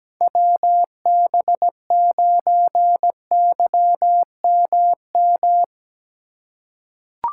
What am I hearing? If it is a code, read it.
Morse code, heard as a keyed tone, about 17 wpm, 700 Hz. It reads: WB9YMM
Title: USA Call Signs - 6 Characters - 2x Character Spacing - 17wpm